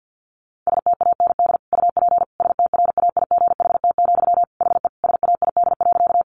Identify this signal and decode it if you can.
Morse code, heard as a keyed tone, about 50 wpm, 700 Hz. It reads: STAND UP STRAIGHT, HE SAID.
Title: Sentences from Top 1000 Words 50wpm